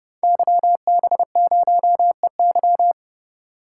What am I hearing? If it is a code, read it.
Morse code, heard as a keyed tone, about 30 wpm, 700 Hz. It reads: YB0EY
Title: Code Groups - 5 Letter:Numbers - 30wpm